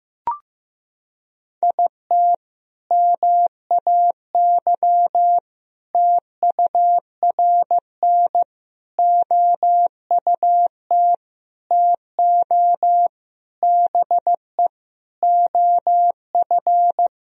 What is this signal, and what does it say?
Morse code, heard as a keyed tone, about 15 wpm, 700 Hz. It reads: IT MAY TURN OUT TO BE OF
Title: Sentences from Top 1000 Words - Review Part 2 - 15wpm